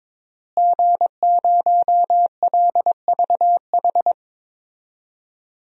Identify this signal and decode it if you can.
Morse code, heard as a keyed tone, about 22 wpm, 700 Hz. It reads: G0LVH
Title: DXCC-100 Call Signs - 5 Characters - Repeated 2x - 22wpm